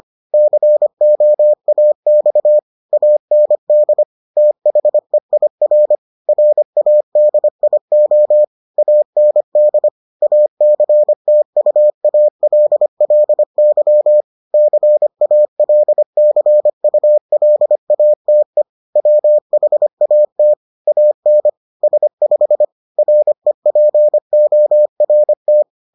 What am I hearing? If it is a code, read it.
Morse code, heard as a keyed tone, about 25 wpm, 600 Hz. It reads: COAX AND THEIR RADIO AND ACTUALLY CALCULATE WHAT AN S5 REPORT